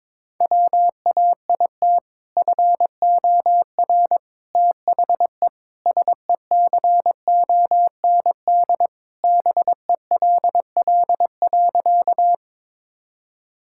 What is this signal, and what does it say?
Morse code, heard as a keyed tone, about 22 wpm, 700 Hz. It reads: WAIT FOR THE SECOND BELL.
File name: practice / 22wpm / Sentences with 2-Word Phrases - 22wpm